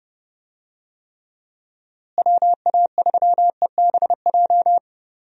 Morse code, heard as a keyed tone, about 30 wpm, 700 Hz. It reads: WA3EBJ
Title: USA Call Signs - 6 Characters - Repeated 3x - 30wpm